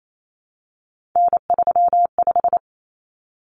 Morse code, heard as a keyed tone, about 28 wpm, 700 Hz. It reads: N35
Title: Common Cut Numbers - 3-Digit 28wpm